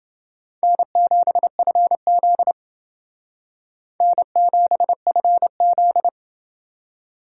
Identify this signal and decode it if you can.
Morse code, heard as a keyed tone, about 30 wpm, 700 Hz. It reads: N7FZ N7FZ